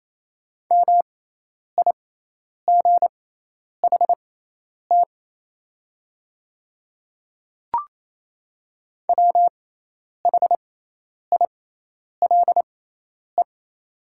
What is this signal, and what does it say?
Morse code, heard as a keyed tone, about 28 wpm, spaced as for 10 wpm, 700 Hz. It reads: MIGHT WHILE
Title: 6x Character Spacing - Top 300 Words - 28wpm